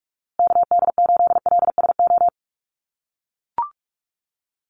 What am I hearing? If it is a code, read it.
Morse code, heard as a keyed tone, about 44 wpm, 700 Hz. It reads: KD8LSO
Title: USA Call Signs Repeated 2x 45wpm